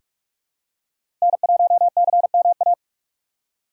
Morse code, heard as a keyed tone, about 47 wpm, 700 Hz. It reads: N1CMA
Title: USA Call Signs - 5 Characters - 4x Character Spacing - 45wpm